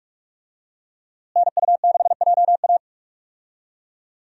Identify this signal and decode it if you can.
Morse code, heard as a keyed tone, about 46 wpm, 700 Hz. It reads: NU6JA
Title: USA Call Signs - 5 Characters - Repeated 2x - 45wpm